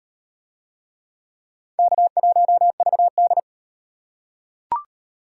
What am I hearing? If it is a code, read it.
Morse code, heard as a keyed tone, about 38 wpm, 700 Hz. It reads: K1VD